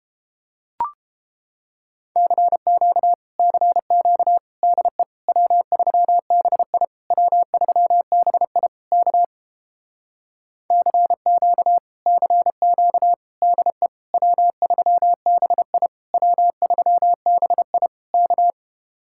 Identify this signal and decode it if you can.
Morse code, heard as a keyed tone, about 32 wpm, 700 Hz. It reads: CQ CQ DE W3BI W3BI K CQ CQ DE W3BI W3BI K